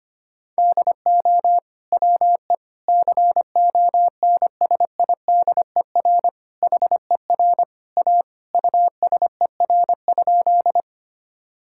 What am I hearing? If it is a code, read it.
Morse code, heard as a keyed tone, about 25 wpm, 700 Hz. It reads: DO WE CONSIDER HER A USER?
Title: Sentences from Top 1000 Words - Review Part 5 - 25wpm